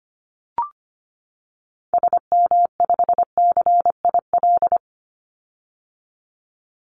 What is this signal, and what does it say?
Morse code, heard as a keyed tone, about 25 wpm, 700 Hz. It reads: SM5CIL
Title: Call Signs - 25wpm